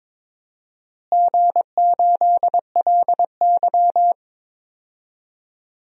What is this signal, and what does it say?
Morse code, heard as a keyed tone, about 21 wpm, 700 Hz. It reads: G8LY
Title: DXCC-100 Call Signs - 4 Characters - Repeated 3x - 22wpm